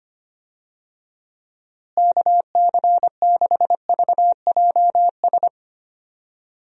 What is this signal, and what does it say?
Morse code, heard as a keyed tone, about 25 wpm, 700 Hz. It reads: KC6VJS
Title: USA Call Signs - 6 Characters - Repeated 2x - 25wpm